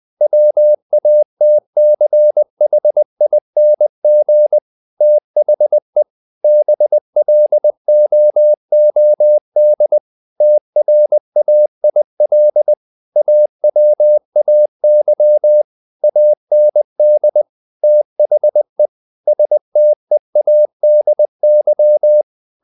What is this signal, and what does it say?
Morse code, heard as a keyed tone, about 20 wpm, 600 Hz. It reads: WATCHING THE BLOOD TRAIL AWAY AND THE STEADY
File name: the old man and the sea / the old man and the sea part 1 20wpm